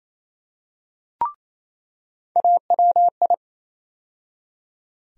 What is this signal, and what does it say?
Morse code, heard as a keyed tone, about 28 wpm, 700 Hz. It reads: AWI
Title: USA Call Sign Postfixes - 3 Letters 28wpm